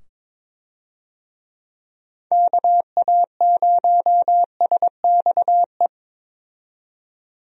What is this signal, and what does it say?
Morse code, heard as a keyed tone, about 22 wpm, 700 Hz. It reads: KA0SXE
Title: USA Call Signs - 6 Characters - 22wpm